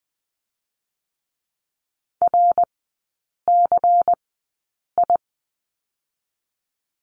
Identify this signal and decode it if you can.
Morse code, heard as a keyed tone, about 19 wpm, spaced as for 8 wpm, 700 Hz. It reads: RCI
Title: 121 Lesson 20e - Number 3 - Three Characters 20wpm